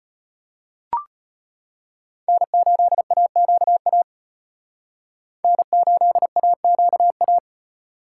Morse code, heard as a keyed tone, about 33 wpm, 700 Hz. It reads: N8AQA N8AQA